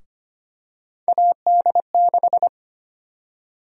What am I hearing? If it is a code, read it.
Morse code, heard as a keyed tone, about 25 wpm, 700 Hz. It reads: AD6
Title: USA Call Sign Prefixes - Repeated 2x - 25wpm